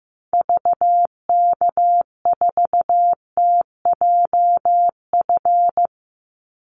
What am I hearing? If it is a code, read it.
Morse code, heard as a keyed tone, about 15 wpm, 700 Hz. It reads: VK4TJF
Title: Call Signs Repeated 3x - 15wpm